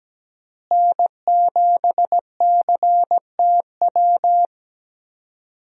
Morse code, heard as a keyed tone, about 17 wpm, 700 Hz. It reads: N7CTW
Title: USA Call Signs - 5 Characters - Repeated 3x - 17wpm